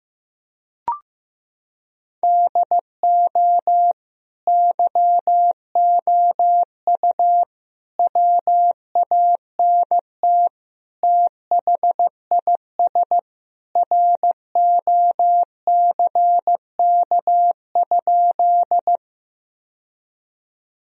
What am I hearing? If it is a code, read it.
Morse code, heard as a keyed tone, about 15 wpm, 700 Hz. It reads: DO YOU WANT THIS ROCK?
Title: Sentences from Top 1000 Words - Review Part 2 - 15wpm